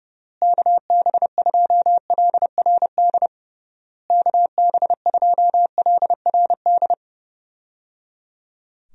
Morse code, heard as a keyed tone, about 30 wpm, 700 Hz. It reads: KB2LRD KB2LRD
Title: USA Call Signs - 6 Characters - Repeated 3x - 30wpm